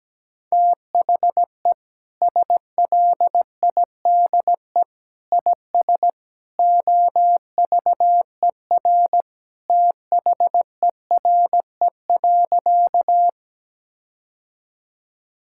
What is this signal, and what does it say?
Morse code, heard as a keyed tone, about 17 wpm, 700 Hz. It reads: THE SLIDE IS OVER THERE.
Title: Sentences from Top 1600 Words 17wpm